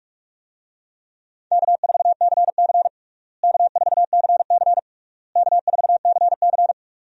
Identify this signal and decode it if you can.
Morse code, heard as a keyed tone, about 41 wpm, 700 Hz. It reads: K4CC K4CC K4CC
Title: USA Call Signs - 4 Characters - Repeated 3x - 45wpm